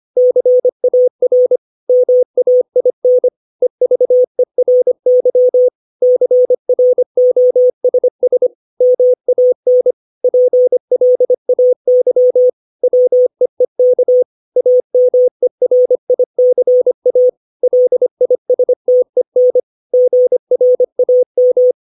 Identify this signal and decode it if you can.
Morse code, heard as a keyed tone, about 25 wpm, 500 Hz. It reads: CAR MAIN EVERY CROSS MAN PLAY WEEK AMERICA LISTEN GRAM
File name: 632 words / 632 Words 25WPM